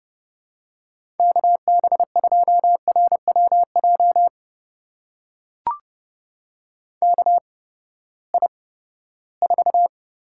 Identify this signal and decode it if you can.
Morse code, heard as a keyed tone, about 30 wpm, spaced as for 17 wpm, 700 Hz. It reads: KB2RWJ KI4
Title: USA Call Signs - 6 Characters - 8x Character Spacing - 30wpm